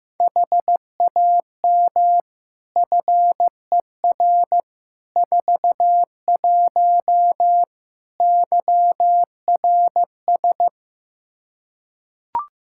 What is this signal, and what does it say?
Morse code, heard as a keyed tone, about 15 wpm, 700 Hz. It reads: HAM FER 41 YRS
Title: Practice QSO - Exchange 3 - More Info - 15wpm